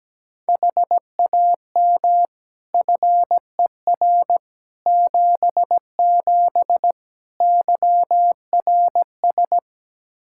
Morse code, heard as a keyed tone, about 17 wpm, 700 Hz. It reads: HAM FER 77 YRS
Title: K1USN SST QSO Practice 17wpm